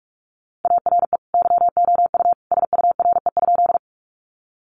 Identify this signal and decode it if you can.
Morse code, heard as a keyed tone, about 44 wpm, 700 Hz. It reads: ARE YOU SURE?
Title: Sentences from Top 300 Words 45wpm